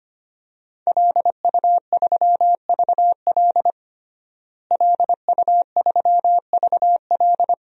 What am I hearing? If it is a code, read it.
Morse code, heard as a keyed tone, about 25 wpm, 700 Hz. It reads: LU3VL LU3VL
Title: DXCC-100 Call Signs - 5 Characters - Repeated 2x - 25wpm